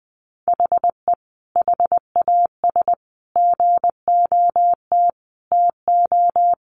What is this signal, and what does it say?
Morse code, heard as a keyed tone, about 20 wpm, 700 Hz. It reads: HE HAS GOT TO
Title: Sentences from Top 800 Words 20wpm